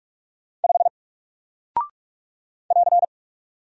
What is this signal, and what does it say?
Morse code, heard as a keyed tone, about 46 wpm, 700 Hz. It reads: 5 <AR>
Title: Single Character (Letter:Number:Punctuation:Prosign) Alt 45wpm